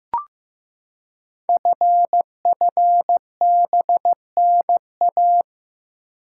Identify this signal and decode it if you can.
Morse code, heard as a keyed tone, about 15 wpm, 700 Hz. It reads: FFBNA
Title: Code Groups - 5 Letters - 15wpm